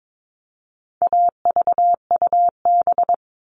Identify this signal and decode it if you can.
Morse code, heard as a keyed tone, about 22 wpm, 700 Hz. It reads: AVUB
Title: All Cut Numbers - 4-Digit 22wpm